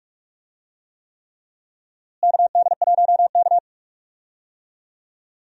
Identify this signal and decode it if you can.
Morse code, heard as a keyed tone, about 47 wpm, 700 Hz. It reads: KD1K
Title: USA Call Signs - 4 Characters - Repeated 3x - 45wpm